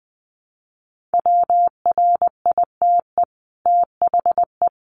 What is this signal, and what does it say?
Morse code, heard as a keyed tone, about 20 wpm, 700 Hz. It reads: WRITE THE
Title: Sentences from Top 1400 Words 20wpm